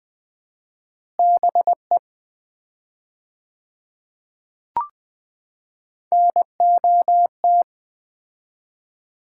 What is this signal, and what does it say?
Morse code, heard as a keyed tone, about 20 wpm, 700 Hz. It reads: BE NOT
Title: 4x Word-Spacing - Sets of 3 Words - 20wpm